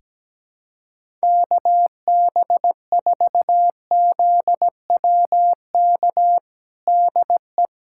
Morse code, heard as a keyed tone, about 17 wpm, 700 Hz. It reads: KB4ZWK DE
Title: USA Callsign DE Callsign 17wpm